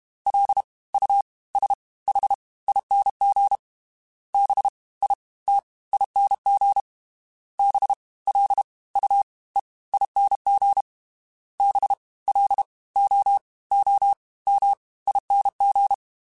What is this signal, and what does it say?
Morse code, heard as a keyed tone, about 32 wpm, 800 Hz. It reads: L U S H ING B I T ING B L U E ING B L O O M ING
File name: Special INGs / Special INGs-mp3s / Fast ing @30